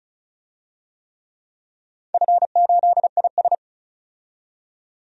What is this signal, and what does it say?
Morse code, heard as a keyed tone, about 36 wpm, 700 Hz. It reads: F8IS
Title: DXCC-100 Call Signs - 4 Characters - 35wpm